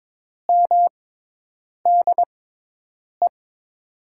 Morse code, heard as a keyed tone, about 21 wpm, spaced as for 8 wpm, 700 Hz. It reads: MDE
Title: USA Call Signs - 6 Characters - 6x Character Spacing - 22wpm